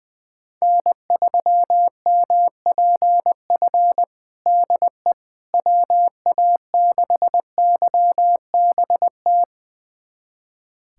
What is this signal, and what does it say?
Morse code, heard as a keyed tone, about 20 wpm, 700 Hz. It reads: N3MPF DE WA6YBT
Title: USA Callsign DE Callsign 20wpm